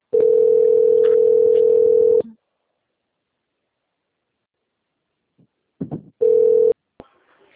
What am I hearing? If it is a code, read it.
Ringback tone (North America), 440 + 480 Hz.
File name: out- - - - - .9066